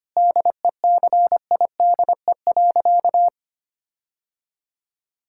Morse code, heard as a keyed tone, about 25 wpm, 700 Hz. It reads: DECIDE.
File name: practice / 25wpm / Sentences with 4-Word Phrases - 25wpm